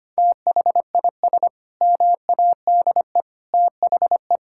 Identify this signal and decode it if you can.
Morse code, heard as a keyed tone, about 25 wpm, 700 Hz. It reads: THIS MADE THE
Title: Sentences from Top 1000 Words - Review Part 6 - 25wpm